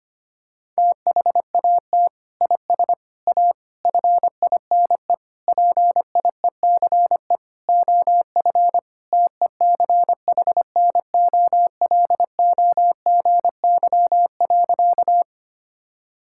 Morse code, heard as a keyed tone, about 25 wpm, 700 Hz. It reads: THAT IS A FINE PIECE OF TECHNOLOGY.
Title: Sentences from Top 1000 Words - Review Part 5 - 25wpm